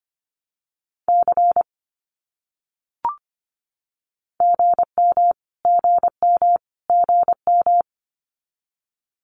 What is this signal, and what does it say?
Morse code, heard as a keyed tone, about 25 wpm, 700 Hz. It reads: C GM GM GM
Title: Abbreviations Repeated 3x - 25wpm